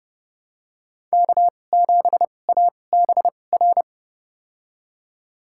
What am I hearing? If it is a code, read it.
Morse code, heard as a keyed tone, about 30 wpm, spaced as for 22 wpm, 700 Hz. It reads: K7ABR
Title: USA Call Signs - 5 Characters - 2x Character Spacing - 30wpm